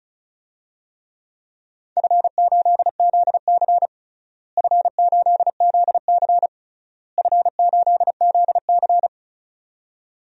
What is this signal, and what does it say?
Morse code, heard as a keyed tone, about 36 wpm, 700 Hz. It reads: F8ZC F8ZC F8ZC
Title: DXCC-100 Call Signs - 4 Characters - Repeated 3x - 35wpm